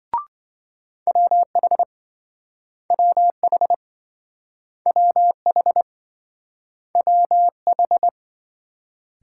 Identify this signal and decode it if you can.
Morse code, heard as a keyed tone, about 27 wpm, 700 Hz. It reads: WH WH WH WH